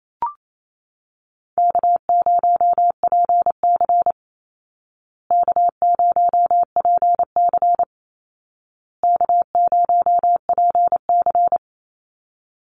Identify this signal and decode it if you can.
Morse code, heard as a keyed tone, about 28 wpm, 700 Hz. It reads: K0PC K0PC K0PC
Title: Easy Call Signs Repeated 3x - 28wpm